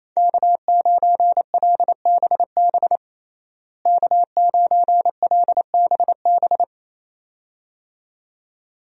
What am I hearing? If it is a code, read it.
Morse code, heard as a keyed tone, about 28 wpm, 700 Hz. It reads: K9LBB K9LBB
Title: USA Call Signs - 5 Characters - Repeated 2x - 28wpm